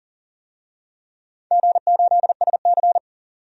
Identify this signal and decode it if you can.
Morse code, heard as a keyed tone, about 37 wpm, 700 Hz. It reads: G8SC
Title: DXCC-100 Call Signs - 4 Characters - 40wpm